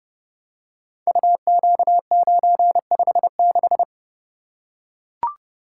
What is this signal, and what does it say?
Morse code, heard as a keyed tone, about 30 wpm, 700 Hz. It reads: UQ956
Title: Code Groups - 5 Letter:Numbers - 30wpm